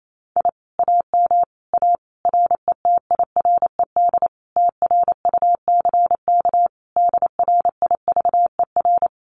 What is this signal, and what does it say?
Morse code, heard as a keyed tone, about 28 wpm, 700 Hz. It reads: I AM A RETIRED TRUCK DRIVER
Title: Common QSO Phrases Encore 28wpm